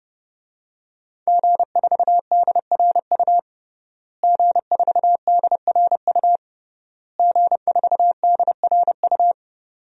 Morse code, heard as a keyed tone, about 32 wpm, 700 Hz. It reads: G4DRU G4DRU G4DRU
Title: DXCC-100 Call Signs - 5 Characters - Repeated 3x - 30wpm